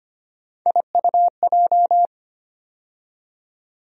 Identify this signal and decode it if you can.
Morse code, heard as a keyed tone, about 25 wpm, 700 Hz. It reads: IUJ
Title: USA Call Sign Postfixes - 3 Letters 25wpm